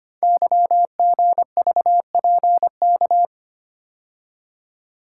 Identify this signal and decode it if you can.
Morse code, heard as a keyed tone, about 25 wpm, 700 Hz. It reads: YGVPK
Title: Code Groups - 5 Letters - 25wpm